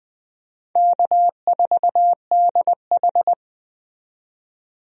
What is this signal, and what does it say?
Morse code, heard as a keyed tone, about 21 wpm, 700 Hz. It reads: K4DH
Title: USA Call Signs - 4 Characters - 20wpm